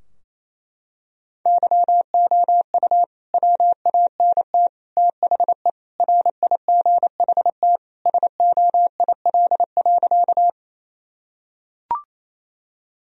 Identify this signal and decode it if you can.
Morse code, heard as a keyed tone, about 27 wpm, 700 Hz. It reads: YOU WANT THE RIGHT SOIL.
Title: Sentences from Top 1000 Words - Review Part 4 - 28wpm